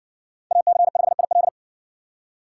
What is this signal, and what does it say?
Morse code, heard as a keyed tone, about 60 wpm, 700 Hz. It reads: AC5IL